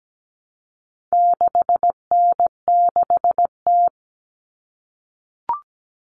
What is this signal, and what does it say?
Morse code, heard as a keyed tone, about 17 wpm, 700 Hz. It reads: 6N6T
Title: All Cut Numbers - 4-Digit 17wpm